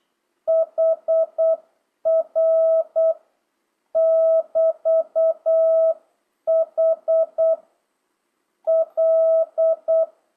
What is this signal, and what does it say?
Morse code, heard as a keyed tone, about 8 wpm, 660 Hz. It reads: HR <BT> H L